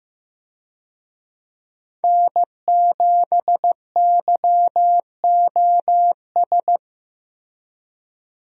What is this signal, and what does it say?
Morse code, heard as a keyed tone, about 15 wpm, 700 Hz. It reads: N7YOS
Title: USA Call Signs - 5 Characters - Repeated 2x - 15wpm